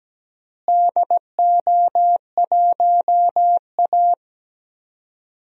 Morse code, heard as a keyed tone, about 17 wpm, 700 Hz. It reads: DO1A
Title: DXCC-100 Call Signs - 4 Characters - Repeated 3x - 17wpm